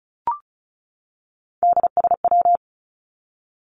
Morse code, heard as a keyed tone, about 34 wpm, 700 Hz. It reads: DSW
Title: Abbreviations 35wpm